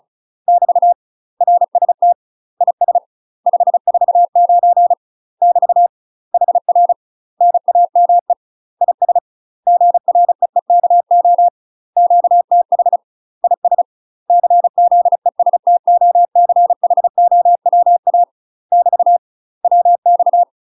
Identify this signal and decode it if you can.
Morse code, heard as a keyed tone, about 36 wpm, 700 Hz. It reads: <BT> RST IS 549 <BT> HR NAME IS GREEKO QTH IS CZESTOCHOWA <BT> WX
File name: UK MASTER QSOs TRAINING / 35WPM / UK MASTER QSOs 0042